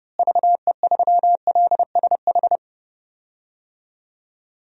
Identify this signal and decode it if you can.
Morse code, heard as a keyed tone, about 30 wpm, 700 Hz. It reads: VE3LSH
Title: DXCC-100 Call Signs - 6 Characters - Repeated 3x - 30wpm